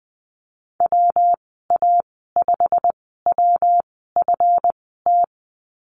Morse code, heard as a keyed tone, about 20 wpm, spaced as for 14 wpm, 700 Hz. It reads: WA5WFT
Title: 140 Lesson 23g - Letter f - Call Signs 20wpm